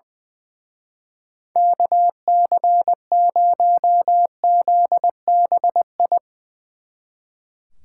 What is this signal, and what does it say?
Morse code, heard as a keyed tone, about 20 wpm, 700 Hz. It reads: KC0ZBI
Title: Speed Racing - USA 2x3 Call Signs 20wpm